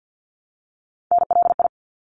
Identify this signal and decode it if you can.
Morse code, heard as a keyed tone, about 50 wpm, 700 Hz. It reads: NLI
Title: ARRL Sections 50wpm